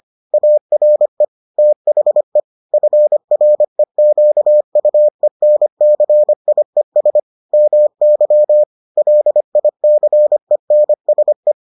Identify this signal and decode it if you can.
Morse code, heard as a keyed tone, about 25 wpm, 600 Hz. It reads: ARE THE FREQUENCIES MY LICENSE